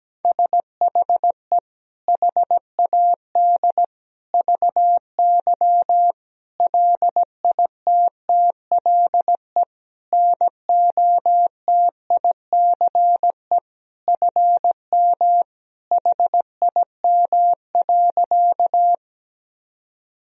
Morse code, heard as a keyed tone, about 17 wpm, 700 Hz. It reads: SHE HAD VY LITTLE NOTICE FM HIM.
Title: Sentences from Top 400 Words and Abbreviations 17wpm